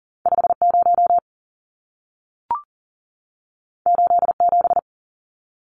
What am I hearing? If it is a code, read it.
Morse code, heard as a keyed tone, about 40 wpm, 700 Hz. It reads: 50 87
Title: Numbers - 2-Digit 40wpm